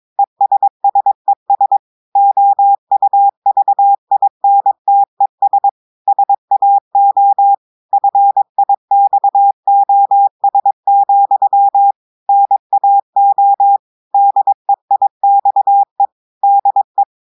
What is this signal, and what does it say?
Morse code, heard as a keyed tone, about 22 wpm, 800 Hz. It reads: ESSES OUVINTES SAO FIXOS, NAO DEIXE DE
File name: audios / portugais / portugais texte en clair 2 vitesse 22 mots minutes